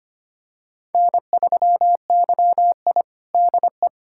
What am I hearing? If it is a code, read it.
Morse code, heard as a keyed tone, about 25 wpm, 700 Hz. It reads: N3YI DE
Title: USA Callsign DE Callsign - 4 Characters - 25wpm